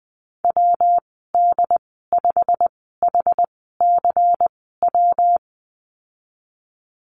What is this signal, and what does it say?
Morse code, heard as a keyed tone, about 20 wpm, spaced as for 15 wpm, 700 Hz. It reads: WD5HCW
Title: 128 Lesson 21g - Number 6 - Call Signs 20wpm